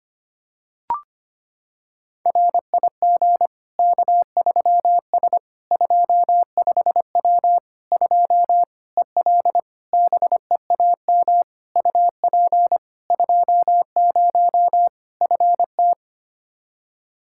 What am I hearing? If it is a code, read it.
Morse code, heard as a keyed tone, about 25 wpm, 700 Hz. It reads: RIG K3S 25W 2 EL BEAM UP 20 FT
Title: Practice QSO - Exchange 2 - Info - 25wpm